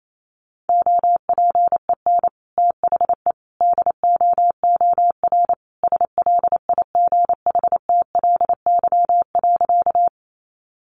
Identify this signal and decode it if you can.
Morse code, heard as a keyed tone, about 28 wpm, 700 Hz. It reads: OPEN THE DOOR SLIGHTLY.
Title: Sentences from Top 1300 Words 28wpm